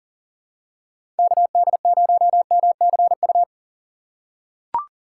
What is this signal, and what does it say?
Morse code, heard as a keyed tone, about 39 wpm, 700 Hz. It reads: KD0MCU
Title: USA Call Signs - 2x Character Spacing - 40wpm